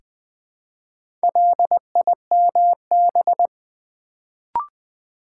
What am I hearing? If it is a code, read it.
Morse code, heard as a keyed tone, about 20 wpm, 700 Hz. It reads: LIMB
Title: Build A Word - 1 Letter At A Time - 4-Letter Words - 20wpm